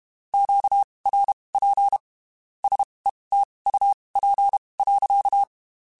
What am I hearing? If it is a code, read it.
Morse code, heard as a keyed tone, about 32 wpm, spaced as for 23 wpm, 800 Hz. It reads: QRP SETUP.
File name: QSOs / QSOs-mp3s / WD0BGP 1@30